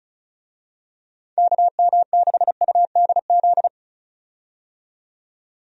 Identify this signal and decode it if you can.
Morse code, heard as a keyed tone, about 34 wpm, 700 Hz. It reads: KM6UDZ
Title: USA Call Signs - 6 Characters - 35wpm